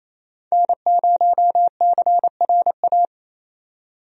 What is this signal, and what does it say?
Morse code, heard as a keyed tone, about 27 wpm, 700 Hz. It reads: N0CRA
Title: USA Call Signs - 5 Characters - 28wpm